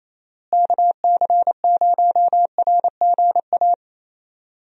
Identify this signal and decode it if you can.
Morse code, heard as a keyed tone, about 28 wpm, 700 Hz. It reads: KC0RGA
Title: USA Call Signs - 6 Characters - Repeated 2x - 28wpm